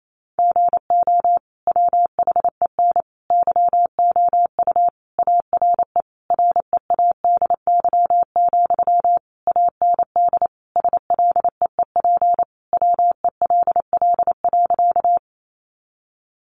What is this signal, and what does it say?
Morse code, heard as a keyed tone, about 28 wpm, 700 Hz. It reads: GO WHEN YOU ARE READY, AND SLEEP WELL.
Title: Sentences from Top 500 Words 28wpm